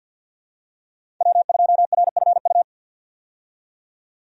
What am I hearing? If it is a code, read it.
Morse code, heard as a keyed tone, about 49 wpm, 700 Hz. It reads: W2RFU
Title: USA Call Signs - 5 Characters - 50wpm